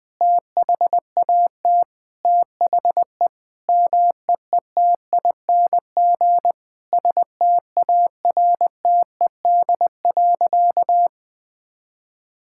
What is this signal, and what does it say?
Morse code, heard as a keyed tone, about 20 wpm, 700 Hz. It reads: THAT THE MEETING STARTED.
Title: Sentences with 5-Word Phrases - Spoken Beginning - 20wpm